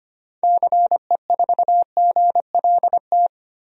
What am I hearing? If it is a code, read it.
Morse code, heard as a keyed tone, about 25 wpm, 690 Hz. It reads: CE4GLT
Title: DXCC-100 Call Signs - 6 Characters - 25wpm